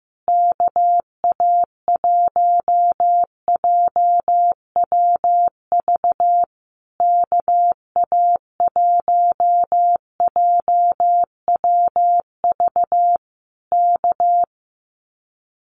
Morse code, heard as a keyed tone, about 15 wpm, 700 Hz. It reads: KA1JWV KA1JWV K
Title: Common QSO Phrases Encore 15wpm